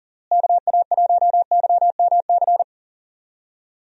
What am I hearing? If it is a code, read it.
Morse code, heard as a keyed tone, about 40 wpm, 700 Hz. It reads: KA1YMC
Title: USA Call Signs - 6 Characters - Repeated 3x - 40wpm